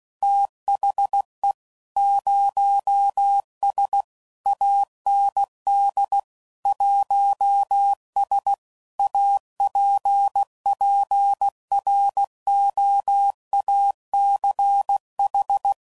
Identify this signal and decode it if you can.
Morse code, heard as a keyed tone, about 16 wpm, 800 Hz. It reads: THE 0S AND 1S APPROACH
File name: CW Code Groups @15WPM / Technical Subjects / Gates Intro